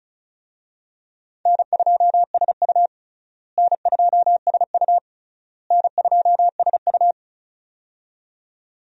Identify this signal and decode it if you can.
Morse code, heard as a keyed tone, about 36 wpm, 700 Hz. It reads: N2SU N2SU N2SU
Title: USA Call Signs - 4 Characters - Repeated 3x - 35wpm